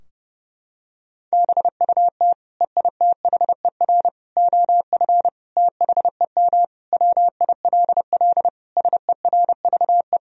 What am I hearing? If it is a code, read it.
Morse code, heard as a keyed tone, about 32 wpm, 700 Hz. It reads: BUT EITHER OF THEM WILL SERVE
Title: Sentences from Top 1000 Words - Review Part 4 - 30wpm